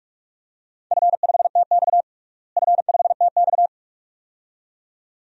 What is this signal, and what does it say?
Morse code, heard as a keyed tone, about 46 wpm, 700 Hz. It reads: F5TX F5TX
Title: DXCC-100 Call Signs - 4 Characters - Repeated 2x - 45wpm